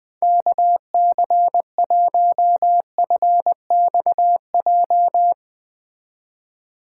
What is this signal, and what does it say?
Morse code, heard as a keyed tone, about 21 wpm, 700 Hz. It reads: KC1FXJ
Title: Speed Racing - USA 2x3 Call Signs 20wpm